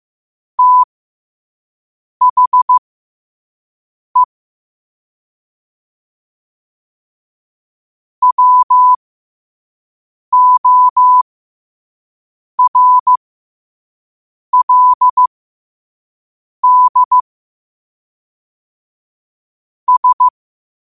Morse code, heard as a keyed tone, about 15 wpm, spaced as for 5 wpm, 1000 Hz. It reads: THE WORLD S
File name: 5wpm